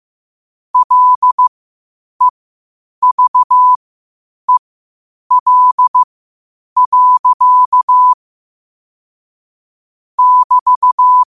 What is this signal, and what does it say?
Morse code, heard as a keyed tone, about 15 wpm, spaced as for 8 wpm, 1000 Hz. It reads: LEVEL. <BT>
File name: qotdfiles / 8wpm